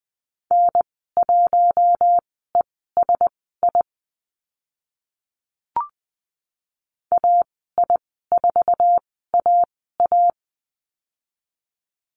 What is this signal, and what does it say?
Morse code, heard as a keyed tone, about 20 wpm, spaced as for 14 wpm, 700 Hz. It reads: N1ESI AI4AA
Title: 062 Lesson 11g - Letter h - Call Signs 20wpm